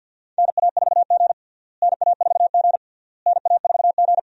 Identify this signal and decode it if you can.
Morse code, heard as a keyed tone, about 52 wpm, 700 Hz. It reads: NA4G NA4G NA4G
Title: USA Call Signs - 4 Characters - Repeated 3x - 50wpm